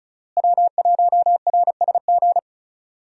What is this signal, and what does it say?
Morse code, heard as a keyed tone, about 36 wpm, 700 Hz. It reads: W1RSG
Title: USA Call Signs - 5 Characters - Repeated 3x - 35wpm